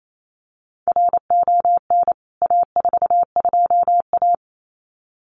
Morse code, heard as a keyed tone, about 28 wpm, 700 Hz. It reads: RON A42A
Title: CWT Exchange 28wpm